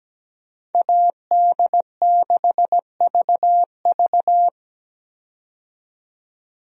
Morse code, heard as a keyed tone, about 17 wpm, 700 Hz. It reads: AD6VV
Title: USA Call Signs - 5 Characters - 17wpm